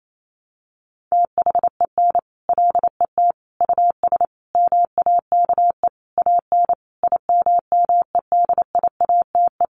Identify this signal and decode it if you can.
Morse code, heard as a keyed tone, about 28 wpm, 700 Hz. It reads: THEN LET US MAKE AN IMMEDIATE
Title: Sentences from Top 1200 Words 28wpm